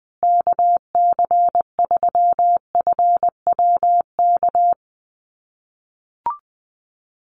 Morse code, heard as a keyed tone, about 20 wpm, 700 Hz. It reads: KC3FWK
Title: USA Call Signs Repeated 2x 20wpm